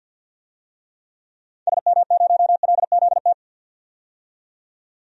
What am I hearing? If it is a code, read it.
Morse code, heard as a keyed tone, about 47 wpm, 700 Hz. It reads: SM0LZT
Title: DXCC-100 Call Signs - 6 Characters - 50wpm